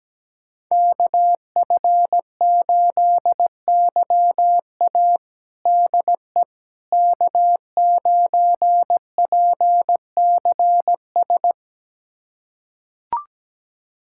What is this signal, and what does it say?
Morse code, heard as a keyed tone, about 17 wpm, 700 Hz. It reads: KF8YA DE K9PCS
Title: USA Callsign DE Callsign - 5 Characters - 17wpm